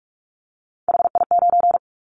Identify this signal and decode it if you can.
Morse code, heard as a keyed tone, about 44 wpm, 700 Hz. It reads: HI9
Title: Callsign Prefixes for Top 100 DXCC - 45wpm